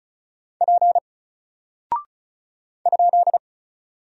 Morse code, heard as a keyed tone, about 36 wpm, 700 Hz. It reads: P ?
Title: Single Character (Letter:Number:Punctuation:Prosign) Alt 35wpm